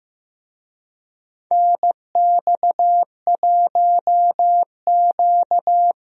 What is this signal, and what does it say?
Morse code, heard as a keyed tone, about 15 wpm, 700 Hz. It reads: NX1Q
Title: USA Call Signs - 4 Characters - Repeated 3x - 15wpm